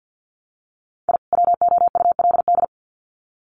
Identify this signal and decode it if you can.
Morse code, heard as a keyed tone, about 50 wpm, 700 Hz. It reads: I WOULD
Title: Sets of 2 Words 50wpm